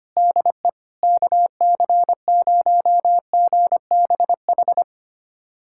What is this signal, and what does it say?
Morse code, heard as a keyed tone, about 25 wpm, 700 Hz. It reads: DE KC0GBH
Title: USA Callsign DE Callsign 25wpm